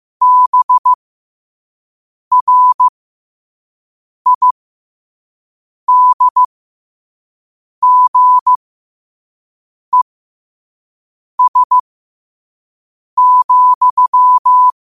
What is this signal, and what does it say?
Morse code, heard as a keyed tone, about 15 wpm, spaced as for 5 wpm, 1000 Hz. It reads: BRIDGES,
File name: qotdfiles / 5wpm